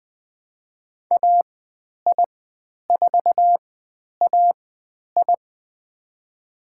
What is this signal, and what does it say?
Morse code, heard as a keyed tone, about 20 wpm, spaced as for 10 wpm, 700 Hz. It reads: AI4AI
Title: 095 Lesson 16g - Letter c - Call Signs 20wpm